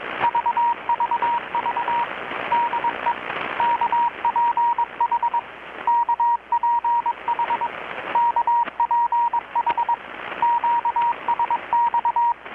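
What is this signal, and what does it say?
Morse code, heard as a keyed tone, about 22 wpm, 940 Hz, with radio noise. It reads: VVV DE KPH KPH KPH QSX
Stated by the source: Morse from KPH's Night of Nights XX maritime radiotelegraphy event, received in Chiapas, Mexico about 3,800 km away on a Kenwood R-600 with a grounded 10 m long-wire antenna.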